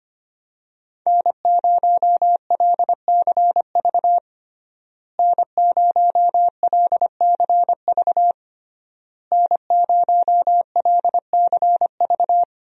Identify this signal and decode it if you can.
Morse code, heard as a keyed tone, about 25 wpm, 700 Hz. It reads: N0LCV N0LCV N0LCV
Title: USA Call Signs - 5 Characters - Repeated 3x - 25wpm